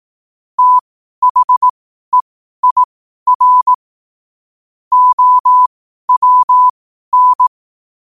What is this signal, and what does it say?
Morse code, heard as a keyed tone, about 18 wpm, spaced as for 12 wpm, 1000 Hz. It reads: THEIR OWN
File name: qotdfiles / 12wpm